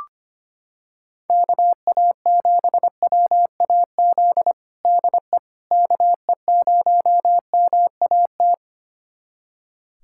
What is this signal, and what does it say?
Morse code, heard as a keyed tone, about 25 wpm, 700 Hz. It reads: KA7WAZ DE KE0MAT
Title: USA Callsign DE Callsign - 6 Characters - 25wpm